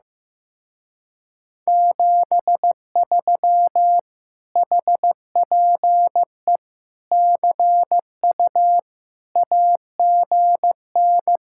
Morse code, heard as a keyed tone, about 15 wpm, 700 Hz. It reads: 73 HPE CU AGN
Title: Speed-Racing Common QSO Phrases Encore - 15wpm